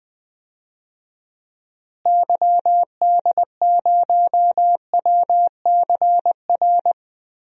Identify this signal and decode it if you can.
Morse code, heard as a keyed tone, about 20 wpm, 700 Hz. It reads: YD0WCR
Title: DXCC-100 Call Signs - 6 Characters - Repeated 3x - 20wpm